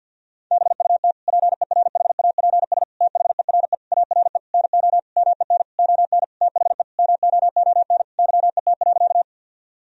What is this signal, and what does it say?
Morse code, heard as a keyed tone, about 50 wpm, 700 Hz. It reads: BUT PERHAPS THERE ARE NO MEN ON THE MOON YET.
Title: Sentences from Top 1000 Words - Review Part 3 - 50wpm